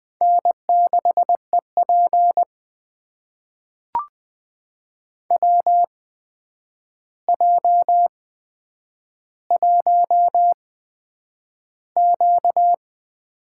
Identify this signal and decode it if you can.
Morse code, heard as a keyed tone, about 21 wpm, spaced as for 9 wpm, 700 Hz. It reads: N6EP WJ1Q
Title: USA Call Signs - 8x Character Spacing - 20wpm